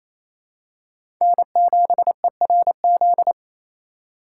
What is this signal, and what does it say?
Morse code, heard as a keyed tone, about 28 wpm, 700 Hz. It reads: N7ERZ
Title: USA Call Signs - 5 Characters - 4x Character Spacing - 28wpm